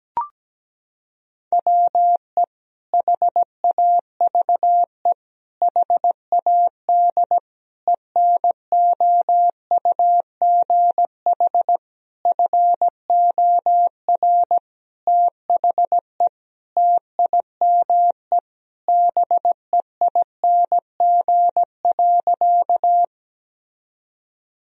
Morse code, heard as a keyed tone, about 17 wpm, 700 Hz. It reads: WE HAVE HAD ENOUGH FOR THE TIME BEING.
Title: Sentences from Top 1000 Words - Review Part 3 - 17wpm